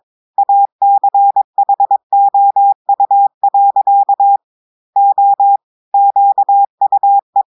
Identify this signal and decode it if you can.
Morse code, heard as a keyed tone, about 22 wpm, 800 Hz. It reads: ACHOU. O QUE
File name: audios / portugais / portugais texte en clair 2 vitesse 22 mots minutes